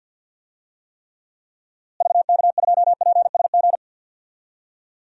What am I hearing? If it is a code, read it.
Morse code, heard as a keyed tone, about 49 wpm, 700 Hz. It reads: VK2PSG
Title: DXCC-100 Call Signs - 6 Characters - Repeated 2x - 50wpm